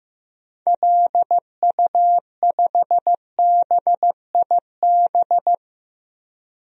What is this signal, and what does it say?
Morse code, heard as a keyed tone, about 15 wpm, 700 Hz. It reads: LU5BIB
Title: DXCC-100 Call Signs - 6 Characters - 15wpm